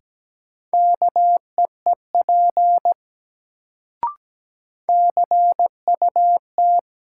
Morse code, heard as a keyed tone, about 17 wpm, 700 Hz. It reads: KEEP CUT
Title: Sentences from Top 300 Words - New Word Drill - 17wpm